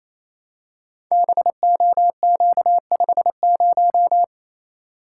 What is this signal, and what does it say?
Morse code, heard as a keyed tone, about 28 wpm, 700 Hz. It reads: BOQ50
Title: Code Groups - 5 Letter:Numbers - 28wpm